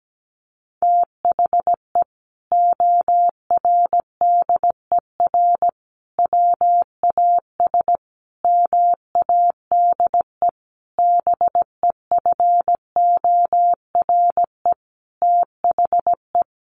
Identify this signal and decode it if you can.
Morse code, heard as a keyed tone, about 17 wpm, 700 Hz. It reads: THE ORDER WAS MADE BEFORE THE
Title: Sentences from Top 1500 Words 17wpm